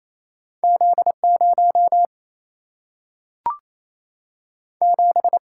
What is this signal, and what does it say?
Morse code, heard as a keyed tone, about 28 wpm, 700 Hz. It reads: Z 0 7
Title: Single-Letter Single-Number 28wpm